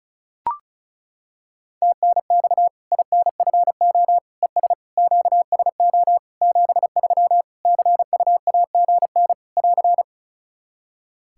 Morse code, heard as a keyed tone, about 36 wpm, 700 Hz. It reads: TNX INFO ES QSO 73 CUAGN <AR>
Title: Practice QSO - Exchange 4 - Ending - 35wpm